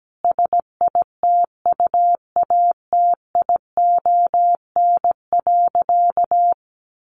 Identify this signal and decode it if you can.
Morse code, heard as a keyed tone, about 17 wpm, 700 Hz. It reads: SITUATION.
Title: Sentences from Top 1200 Words 17wpm